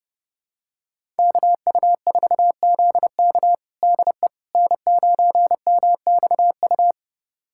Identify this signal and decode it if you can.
Morse code, heard as a keyed tone, about 30 wpm, 700 Hz. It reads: KU4ZK DE N9MXU
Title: USA Callsign DE Callsign - 5 Characters - 30wpm